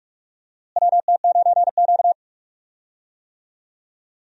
Morse code, heard as a keyed tone, about 46 wpm, 700 Hz. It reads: WT9Q
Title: USA Call Signs - 4 Characters - Repeated 3x - 45wpm